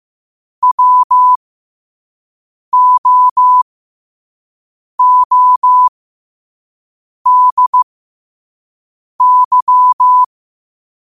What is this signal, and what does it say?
Morse code, heard as a keyed tone, about 15 wpm, spaced as for 5 wpm, 1000 Hz. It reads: WOODY
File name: qotdfiles / 5wpm